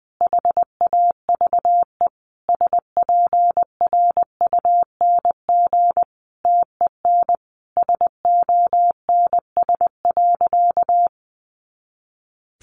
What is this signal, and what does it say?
Morse code, heard as a keyed tone, about 20 wpm, 700 Hz. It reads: HAVE SPRUNG TEN SONS.
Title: A Sound Of Thunder 20wpm